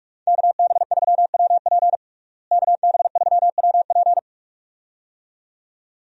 Morse code, heard as a keyed tone, about 46 wpm, 700 Hz. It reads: KB3WP KB3WP
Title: USA Call Signs - 5 Characters - Repeated 2x - 45wpm